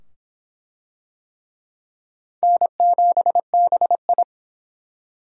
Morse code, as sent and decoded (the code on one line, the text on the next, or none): -. --... -... ..
N7BI